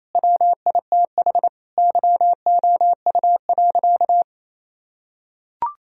Morse code, heard as a keyed tone, about 27 wpm, 700 Hz. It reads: WITH YOU.
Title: Sentences from Top 1000 Words - Review Part 4 - 28wpm